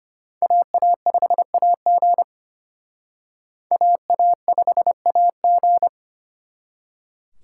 Morse code, heard as a keyed tone, about 31 wpm, 700 Hz. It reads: AA5AG AA5AG
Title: Speed Racing - USA Call Signs 25wpm